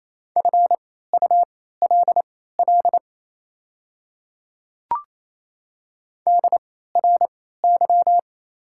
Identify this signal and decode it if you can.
Morse code, heard as a keyed tone, about 28 wpm, spaced as for 16 wpm, 690 Hz. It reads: FULL DRY
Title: 3x Character Spacing - Top 500 Words - 28wpm